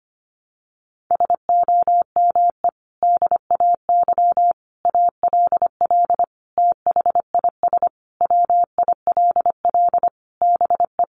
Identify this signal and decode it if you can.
Morse code, heard as a keyed tone, about 25 wpm, 700 Hz. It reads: SOME DAY ALL THIS WILL BE
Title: Sentences from Top 1300 Words 25wpm